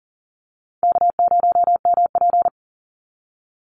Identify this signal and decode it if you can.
Morse code, heard as a keyed tone, about 40 wpm, 700 Hz. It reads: K0MP
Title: Easy Call Signs - 40wpm